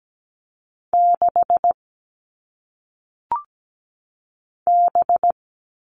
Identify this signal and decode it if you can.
Morse code, heard as a keyed tone, about 17 wpm, 700 Hz. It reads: 6 B
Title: All Cut Numbers - 1-Digit 17wpm